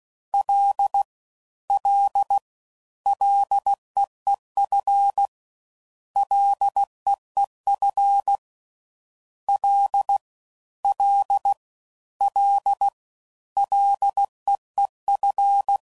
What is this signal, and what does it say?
Morse code, heard as a keyed tone, about 16 wpm, 800 Hz. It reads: L L LEEF LEEF L L L LEEF